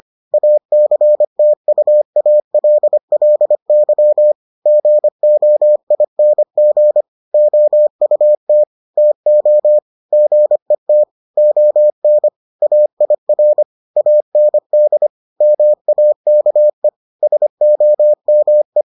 Morse code, heard as a keyed tone, about 25 wpm, 600 Hz. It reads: ACTUALLY GOING OUT TO GET ON AIR AND MAKE SOME